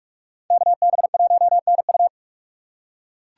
Morse code, heard as a keyed tone, about 47 wpm, 700 Hz. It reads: KB1NU
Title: USA Call Signs - 5 Characters - 2x Character Spacing - 45wpm